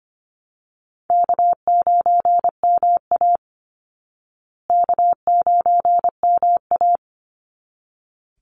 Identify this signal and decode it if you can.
Morse code, heard as a keyed tone, about 25 wpm, 700 Hz. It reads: K9MA K9MA
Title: Call Signs Repeated 3x - 25wpm